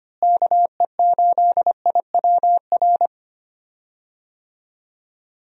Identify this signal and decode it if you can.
Morse code, heard as a keyed tone, about 25 wpm, 700 Hz. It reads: KE8IWR
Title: USA Call Signs - 6 Characters - Repeated 2x - 25wpm